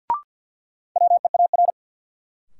Morse code, heard as a keyed tone, about 48 wpm, 700 Hz. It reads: WEAR
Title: Sentences from Top 1000 Words - Review Part 6 - New Word Drill - 50wpm